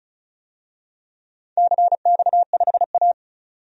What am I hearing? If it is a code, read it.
Morse code, heard as a keyed tone, about 36 wpm, 700 Hz. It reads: CX5A
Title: DXCC-100 Call Signs - 4 Characters - 35wpm